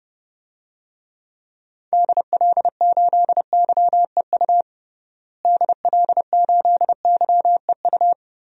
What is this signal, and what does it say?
Morse code, heard as a keyed tone, about 32 wpm, 700 Hz. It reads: DL8YEU DL8YEU
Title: DXCC-100 Call Signs - 6 Characters - Repeated 2x - 30wpm